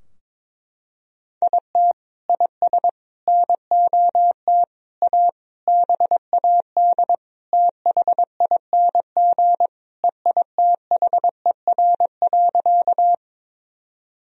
Morse code, heard as a keyed tone, about 22 wpm, 700 Hz. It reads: IT IS NOT A BAD THING EITHER.
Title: Sentences from Top 1000 Words - Review Part 4 - 22wpm